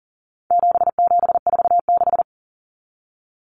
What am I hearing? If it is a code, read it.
Morse code, heard as a keyed tone, about 40 wpm, 700 Hz. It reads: 7746
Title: Numbers - 4-Digit 40wpm